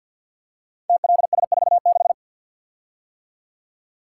Morse code, heard as a keyed tone, about 52 wpm, 700 Hz. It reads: TLS46
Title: Code Groups - 5 Letter:Numbers - 50wpm